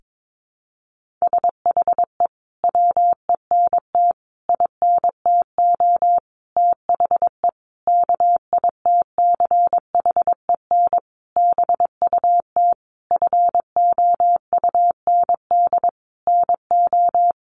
Morse code, heard as a keyed tone, about 22 wpm, 700 Hz. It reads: SHE WENT INTO THE KITCHEN BUT FOUND NO
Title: Sentences from Top 1400 Words 22wpm